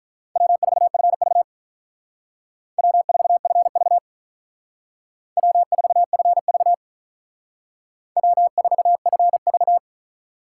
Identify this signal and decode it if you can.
Morse code, heard as a keyed tone, about 45 wpm, 700 Hz. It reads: W4FV W4FV W4FV W4FV